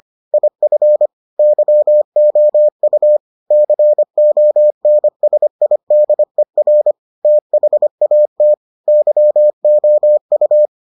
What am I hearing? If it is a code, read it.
Morse code, heard as a keyed tone, about 25 wpm, 600 Hz. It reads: IF YOU CONSIDER THAT YOU